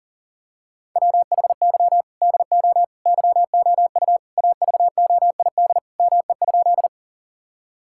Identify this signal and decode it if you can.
Morse code, heard as a keyed tone, about 40 wpm, 700 Hz. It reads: WHY DO YOU AVOID ME?
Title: Sentences from Top 1000 Words - Review Part 6 - 40wpm